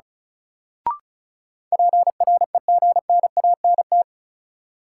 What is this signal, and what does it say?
Morse code, heard as a keyed tone, about 34 wpm, 700 Hz. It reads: PREGNANT
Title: Sentences from Top 1000 Words - Review Part 6 - New Word Drill - 35wpm